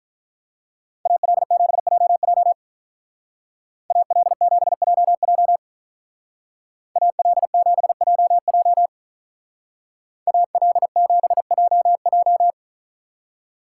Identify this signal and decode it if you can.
Morse code, heard as a keyed tone, about 43 wpm, 700 Hz. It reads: AL7JJ AL7JJ AL7JJ AL7JJ
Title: Speed Racing - USA 2x2 Call Signs 35wpm